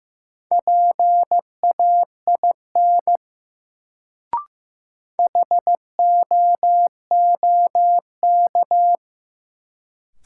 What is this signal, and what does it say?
Morse code, heard as a keyed tone, about 15 wpm, 700 Hz. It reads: PAIN HOOK
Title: Sentences from Top 1600 Words - New Word Drill - 15wpm